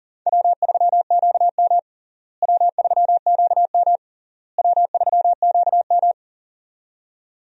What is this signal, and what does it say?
Morse code, heard as a keyed tone, about 37 wpm, 700 Hz. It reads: W3QM W3QM W3QM
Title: USA Call Signs - 4 Characters - Repeated 3x - 40wpm